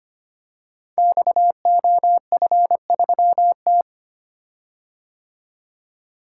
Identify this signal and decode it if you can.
Morse code, heard as a keyed tone, about 25 wpm, 700 Hz. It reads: XOF3T
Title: Code Groups - 5 Letter:Numbers - 25wpm